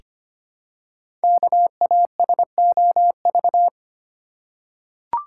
Morse code, heard as a keyed tone, about 25 wpm, 700 Hz. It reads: KASOV
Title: Code Groups - 5 Letters - 25wpm